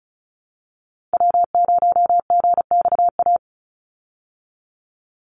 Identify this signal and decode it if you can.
Morse code, heard as a keyed tone, about 34 wpm, 700 Hz. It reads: W0GXA
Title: Call Signs Repeated 2x - 35wpm